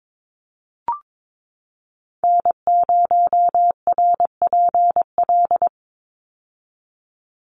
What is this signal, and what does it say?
Morse code, heard as a keyed tone, about 22 wpm, 700 Hz. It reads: N0RPL
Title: USA Call Signs 22wpm